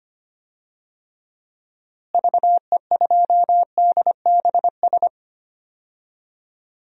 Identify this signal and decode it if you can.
Morse code, heard as a keyed tone, about 25 wpm, 700 Hz. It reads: VE2DBS
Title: DXCC-100 Call Signs - 6 Characters - Repeated 2x - 25wpm